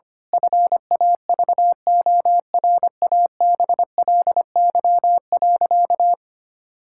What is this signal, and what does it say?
Morse code, heard as a keyed tone, about 25 wpm, 700 Hz. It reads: FAVORABLY.
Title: Pangram Challenge v2 - 25wpm